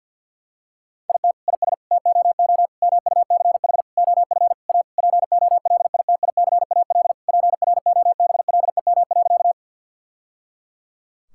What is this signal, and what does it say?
Morse code, heard as a keyed tone, about 50 wpm, 700 Hz. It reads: IT IS TOO MUCH OF A POLITICAL PROBLEM.
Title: Sentences from Top 1000 Words - Review Part 5 - 50wpm